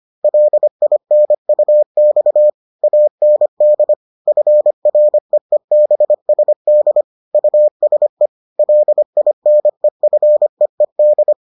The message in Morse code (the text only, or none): LINUX AND FREEBSD USE LINEFEED